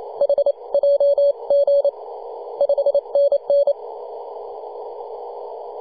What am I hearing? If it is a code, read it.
Morse code, heard as a keyed tone, about 28 wpm, spaced as for 25 wpm, 570 Hz. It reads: HJG 5NN
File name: 2015/May/15 2222z PW0F 21.026MHz CW